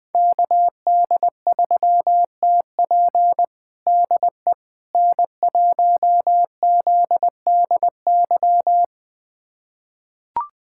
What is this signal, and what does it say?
Morse code, heard as a keyed tone, about 21 wpm, 700 Hz. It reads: KD3TP DE N1ZDY
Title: USA Callsign DE Callsign - 5 Characters - 20wpm